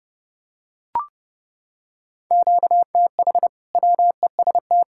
Morse code, heard as a keyed tone, about 30 wpm, 700 Hz. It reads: QTH WEST
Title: Practice QSO - Exchange 1 - Beginning - 30wpm